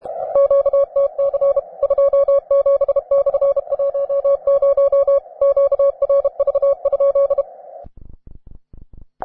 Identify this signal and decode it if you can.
Morse code, heard as a keyed tone, about 34 wpm, 560 Hz. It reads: QTC 27/10 QRV?